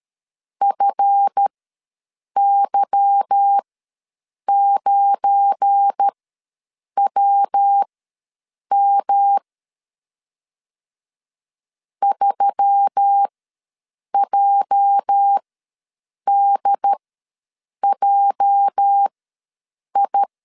Morse code, heard as a keyed tone, about 13 wpm, spaced as for 7 wpm, 800 Hz. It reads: FY9WM 3JDJI
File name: LEZIONI / 8 YJ45 / 8 YJ ws 12wpm 3342